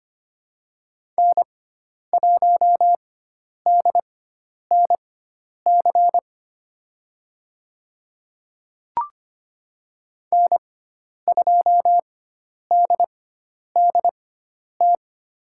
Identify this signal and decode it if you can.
Morse code, heard as a keyed tone, about 25 wpm, spaced as for 10 wpm, 700 Hz. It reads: N1DNC N2DDT
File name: Lesson 16g - Letter c - Call Signs 25wpm